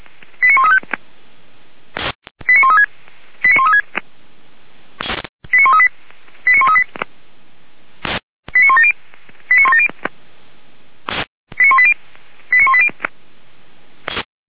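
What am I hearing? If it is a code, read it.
Five-tone selective call (ZVEI), heard as quick runs of five tones, tones of about 70 ms. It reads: 89135 89136 89136 89138 89138 89170 89170 89180 89180
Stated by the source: ZVEI five-tone call 89134 received by SDR.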